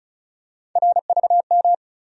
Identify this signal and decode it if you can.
Morse code, heard as a keyed tone, about 36 wpm, 700 Hz. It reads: RVM
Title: USA Call Sign Postfixes - 3 Letters 35wpm